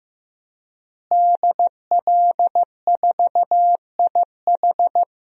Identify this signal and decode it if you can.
Morse code, heard as a keyed tone, about 15 wpm, 700 Hz. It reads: DL4IH
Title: DXCC-100 Call Signs - 5 Characters - Repeated 3x - 15wpm